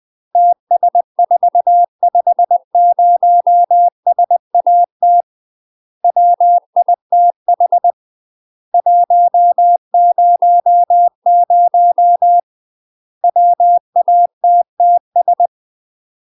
Morse code, heard as a keyed tone, about 20 wpm, 700 Hz. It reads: TS450SAT WITH 100 WATTS